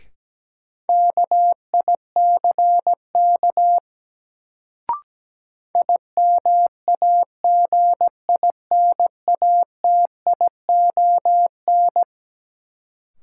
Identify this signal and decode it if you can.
Morse code, heard as a keyed tone, about 17 wpm, 700 Hz. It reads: KICK IMAGINATION
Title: Sentences from Top 1300 Words - New Word Drill - 17wpm